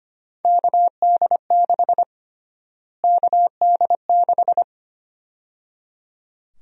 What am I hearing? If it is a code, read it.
Morse code, heard as a keyed tone, about 25 wpm, 700 Hz. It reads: KD6 KD6
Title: USA Call Sign Prefixes - Repeated 2x - 25wpm